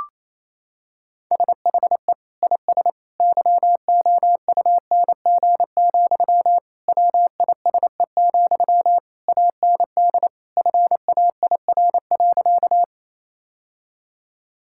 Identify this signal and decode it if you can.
Morse code, heard as a keyed tone, about 28 wpm, 700 Hz. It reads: SHE IS YOUNG, WISE, AND FAIR.
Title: Sentences from Top 1600 Words 28wpm